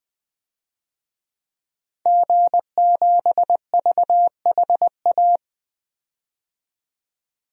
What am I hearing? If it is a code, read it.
Morse code, heard as a keyed tone, about 21 wpm, 700 Hz. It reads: G7VHA
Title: DXCC-100 Call Signs - 5 Characters - 20wpm